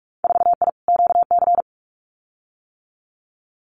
Morse code, heard as a keyed tone, about 46 wpm, 700 Hz. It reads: 4I QC
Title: Winter Field Day Exchanges Encore 45wpm